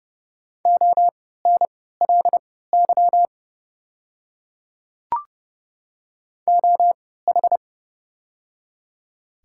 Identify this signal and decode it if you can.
Morse code, heard as a keyed tone, about 29 wpm, spaced as for 17 wpm, 700 Hz. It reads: ONLY OH
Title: 3x Character Spacing - Top 500 Words - 30wpm